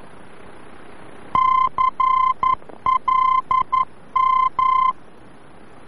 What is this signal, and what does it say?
Morse code, heard as a keyed tone, about 11 wpm, 1040 Hz. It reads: CLM
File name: france / mx / CLM-VOR-F